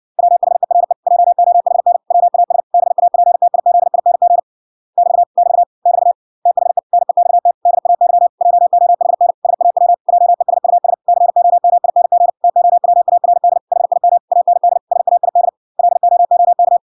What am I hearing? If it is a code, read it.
Morse code, heard as a keyed tone, about 56 wpm, 700 Hz. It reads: WHERE JOHN WAS BAPTIZING <BT> <BT> <BT> THE NEXT DAY JOHN SAW JESUS COMING TOWARD HIM AND SAID LOOK